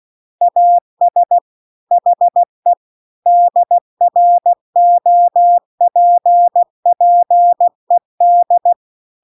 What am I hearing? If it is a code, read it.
Morse code, heard as a keyed tone, about 16 wpm, 700 Hz. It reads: AS HE DROPPED
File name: CW-The-Big-Twoheaded-River-2-16WPM